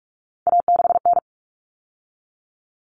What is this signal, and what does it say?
Morse code, heard as a keyed tone, about 44 wpm, 700 Hz. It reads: A6N
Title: All Cut Numbers - 3-Digit 45wpm